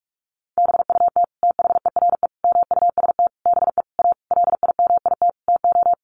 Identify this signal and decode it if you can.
Morse code, heard as a keyed tone, about 44 wpm, 700 Hz. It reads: BUT THERE MUST BE A LIMIT TO
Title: Sentences from Top 1500 Words 45wpm